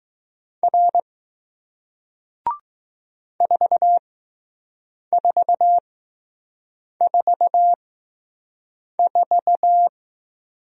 Morse code, heard as a keyed tone, about 20 wpm, spaced as for 6 wpm, 700 Hz. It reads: R 4444